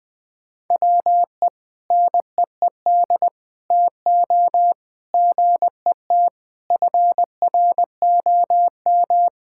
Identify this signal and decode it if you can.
Morse code, heard as a keyed tone, about 21 wpm, 700 Hz. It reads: WE NEED TO GET FROM
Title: Sentences with 6-Word Phrases - 20wpm